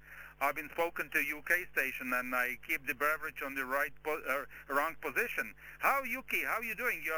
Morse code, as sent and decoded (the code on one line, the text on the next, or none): none